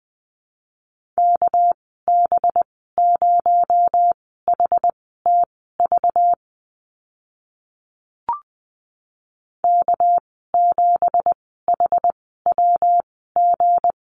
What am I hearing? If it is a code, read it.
Morse code, heard as a keyed tone, about 20 wpm, spaced as for 14 wpm, 700 Hz. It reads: KB0HTV K7HWG
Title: 238 Lesson 38g - Letter q - Call Signs 20wpm